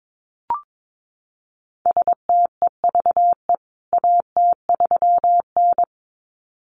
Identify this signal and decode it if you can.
Morse code, heard as a keyed tone, about 22 wpm, 700 Hz. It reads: STEVE AT3N
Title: CWT Exchange 22wpm